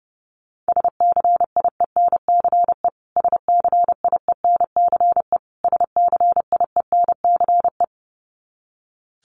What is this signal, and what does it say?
Morse code, heard as a keyed tone, about 30 wpm, 700 Hz. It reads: SCIENCE SCIENCE SCIENCE
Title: Top 500 Words Repeated 30wpm